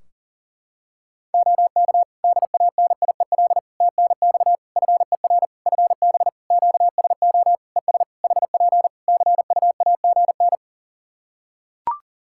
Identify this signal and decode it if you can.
Morse code, heard as a keyed tone, about 40 wpm, 700 Hz. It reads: OK DANIEL TNX FER FB QSO ES HP CUAGN
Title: Practice QSO - Exchange 4 - Ending - 40wpm